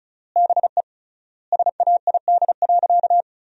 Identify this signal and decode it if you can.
Morse code, heard as a keyed tone, about 34 wpm, 700 Hz. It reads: BE SAID.
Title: 3x Word-Spacing - Sentences from Top 100 Words - 35wpm